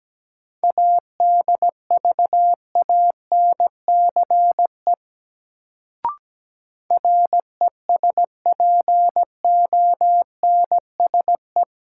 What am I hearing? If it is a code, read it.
Morse code, heard as a keyed tone, about 17 wpm, 700 Hz. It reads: ADVANCE RESPONSE
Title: Sentences from Top 1400 Words - New Word Drill - 17wpm